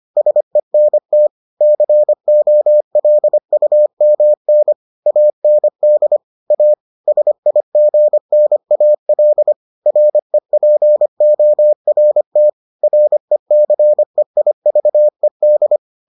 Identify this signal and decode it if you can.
Morse code, heard as a keyed tone, about 25 wpm, 600 Hz. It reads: SENT COLUMN AND A SIGNAL REPORT RECEIVED